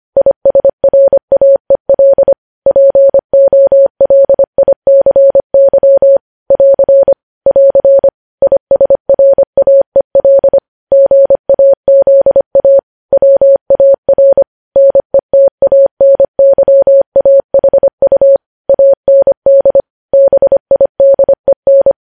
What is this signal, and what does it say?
Morse code, heard as a keyed tone, about 25 wpm, 550 Hz. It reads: ISRAEL POLICY <AR> <AR> ISRAEL GAZA WAR NETANYAHU AND BIDEN
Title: News Headlines at 25